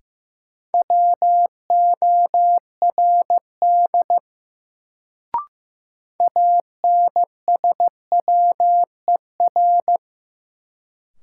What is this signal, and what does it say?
Morse code, heard as a keyed tone, about 15 wpm, 700 Hz. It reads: WORD ANSWER
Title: Sentences from Top 300 Words - New Word Drill - 15wpm